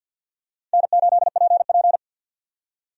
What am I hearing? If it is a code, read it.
Morse code, heard as a keyed tone, about 52 wpm, 700 Hz. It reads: N8PP
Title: USA Call Signs - 4 Characters - 50wpm